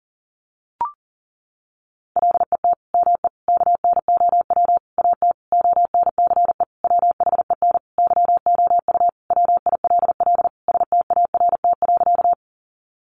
Morse code, heard as a keyed tone, about 40 wpm, 700 Hz. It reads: LET ME KNOW AT ONCE WHEN YOU WILL START.
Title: Sentences from Top 300 Words 40wpm